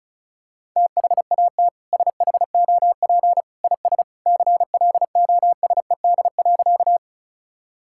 Morse code, heard as a keyed tone, about 34 wpm, 700 Hz. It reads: THAT SHOP IS CLOSED.
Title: Sentences from Top 1600 Words 35wpm